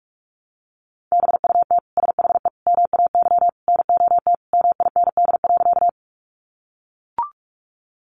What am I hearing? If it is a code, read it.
Morse code, heard as a keyed tone, about 44 wpm, 700 Hz. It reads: BUT SHE MAY NOT MIND.
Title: Sentences from Top 600 Words 45wpm